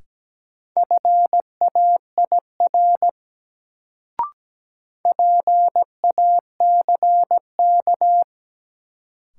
Morse code, heard as a keyed tone, about 17 wpm, 700 Hz. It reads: FAIR PACK
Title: Sentences from Top 1300 Words - New Word Drill Alt - 17wpm